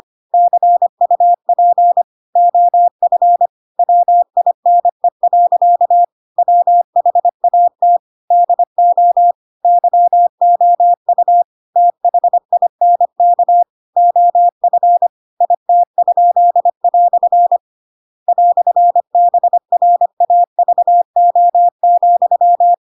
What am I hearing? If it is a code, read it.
Morse code, heard as a keyed tone, about 25 wpm, 700 Hz. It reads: CUP OF WINE. WHAT DO YOU THINK OF IT?" "BRAVO,
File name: Pinocchio / Pinocchio ch2 25 25 ik1hge